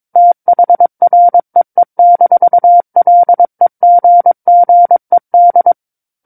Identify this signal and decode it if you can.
Morse code, heard as a keyed tone, about 22 wpm, 700 Hz. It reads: THREE-LEGGED
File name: 06-Chapter-06 22 WPM